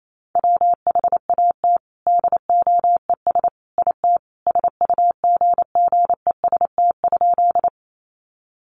Morse code, heard as a keyed tone, about 28 wpm, 700 Hz. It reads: WHAT DOES IT SUGGEST?
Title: Sentences from Top 1300 Words 28wpm